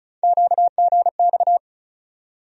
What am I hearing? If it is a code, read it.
Morse code, heard as a keyed tone, about 34 wpm, 700 Hz. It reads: QGX
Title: USA Call Sign Postfixes - 3 Letters 35wpm